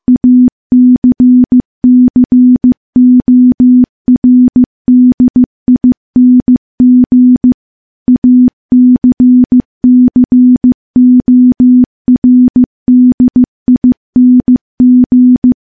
Morse code, heard as a keyed tone, about 15 wpm, 250 Hz. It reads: ACCORDING ACCORDING
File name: morse / morse-training-15WPM